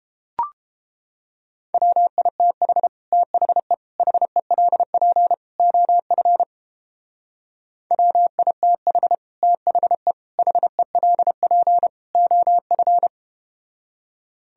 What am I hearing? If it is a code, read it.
Morse code, heard as a keyed tone, about 31 wpm, 700 Hz. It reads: WITH THE HELP OF WITH THE HELP OF